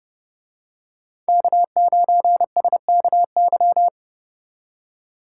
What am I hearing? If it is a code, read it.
Morse code, heard as a keyed tone, about 31 wpm, 700 Hz. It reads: K9SKY